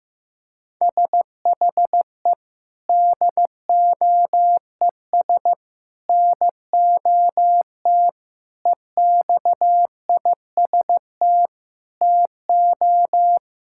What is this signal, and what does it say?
Morse code, heard as a keyed tone, about 15 wpm, 700 Hz. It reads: SHE DOES NOT EXIST TO
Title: Sentences from Top 1000 Words - Review Part 5 - 15wpm